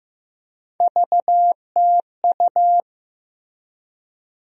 Morse code, heard as a keyed tone, about 15 wpm, 700 Hz. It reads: VTU
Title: USA Call Sign Postfixes - 3 Letters 15wpm